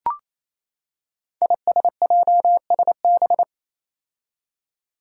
Morse code, heard as a keyed tone, about 28 wpm, 700 Hz. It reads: ISJSB
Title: Five Challenging Letter-Number Combinations 28wpm